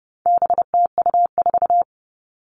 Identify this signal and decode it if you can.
Morse code, heard as a keyed tone, about 30 wpm, 700 Hz. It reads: BTU4
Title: All Cut Numbers - 4-Digit 30wpm